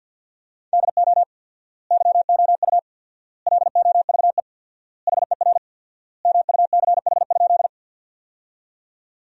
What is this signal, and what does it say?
Morse code, heard as a keyed tone, about 52 wpm, 700 Hz. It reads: DO YOU LOVE HER MUCH?
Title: 4x Word-Spacing - Sentences from Top 400 Words - 50wpm